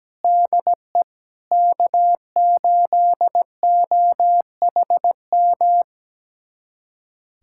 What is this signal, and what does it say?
Morse code, heard as a keyed tone, about 17 wpm, 700 Hz. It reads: DE K8OHM
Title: USA Callsign DE Callsign - 5 Characters - 17wpm